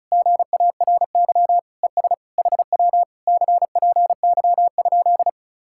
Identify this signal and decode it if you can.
Morse code, heard as a keyed tone, about 40 wpm, 700 Hz. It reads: GARY ES HW CPY?